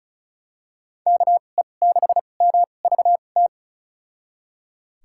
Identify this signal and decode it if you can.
Morse code, heard as a keyed tone, about 36 wpm, spaced as for 25 wpm, 700 Hz. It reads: KE6MVT
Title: USA Call Signs - 6 Characters - 2x Character Spacing - 35wpm